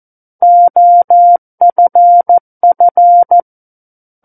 Morse code, heard as a keyed tone, about 14 wpm, 700 Hz. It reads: OFF
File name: Book One / 05-Chapter 5 - 14 WPM